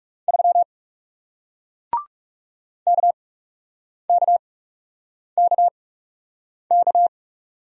Morse code, heard as a keyed tone, about 40 wpm, spaced as for 8 wpm, 700 Hz. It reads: 3 KKKK